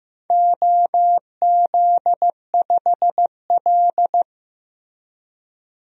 Morse code, heard as a keyed tone, about 15 wpm, 700 Hz. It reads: OZ5L
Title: DXCC-100 Call Signs - 4 Characters - Repeated 2x - 15wpm